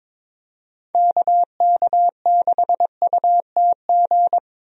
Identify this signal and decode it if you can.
Morse code, heard as a keyed tone, about 22 wpm, 700 Hz. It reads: KK6UTG
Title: USA Call Signs - 6 Characters - Repeated 3x - 22wpm